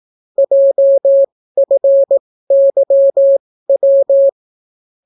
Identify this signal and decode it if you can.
Morse code, heard as a keyed tone, about 18 wpm, spaced as for 14 wpm, 550 Hz. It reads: JFYW